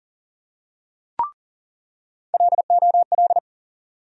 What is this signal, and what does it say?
Morse code, heard as a keyed tone, about 40 wpm, 700 Hz. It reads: LOL
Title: USA Call Sign Postfixes - 3 Letters 40wpm